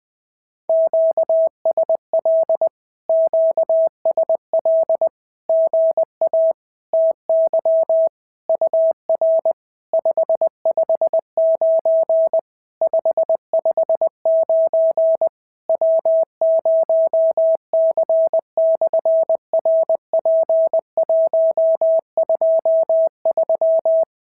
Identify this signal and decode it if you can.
Morse code, heard as a keyed tone, about 20 wpm, 690 Hz. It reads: QSL QSL GA TY UR 559 559 W0C/RP123
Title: SOTA QSO Practice 20wpm